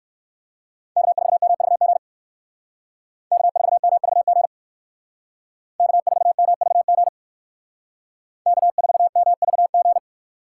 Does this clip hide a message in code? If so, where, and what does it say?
Yes: Morse code from 1.0 to 10.0 s, heard as a keyed tone, about 58 wpm, 700 Hz. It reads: K4MVG K4MVG K4MVG K4MVG